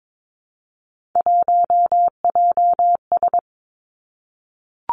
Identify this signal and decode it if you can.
Morse code, heard as a keyed tone, about 22 wpm, 700 Hz. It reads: 1JS
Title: Three Challenging Letter-Number Combinations 22wpm